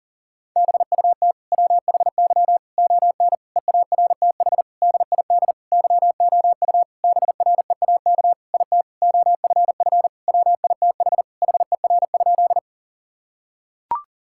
Morse code, heard as a keyed tone, about 40 wpm, 700 Hz. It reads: BUT WHY ON EARTH DID YOU BREAK IT OFF WITH HER?
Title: Sentences from Top 1000 Words - Review Part 6 - 40wpm